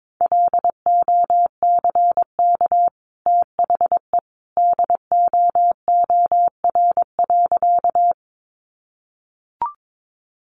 Morse code, heard as a keyed tone, about 22 wpm, 700 Hz. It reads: LOCK THE DOOR.
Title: Sentences from Top 1300 Words 22wpm